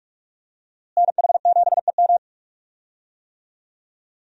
Morse code, heard as a keyed tone, about 46 wpm, 700 Hz. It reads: NH7EM
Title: USA Call Signs - 5 Characters - Repeated 3x - 45wpm